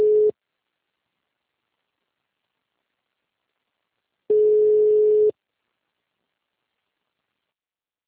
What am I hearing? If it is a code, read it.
Ringback tone (Europe, 425 Hz).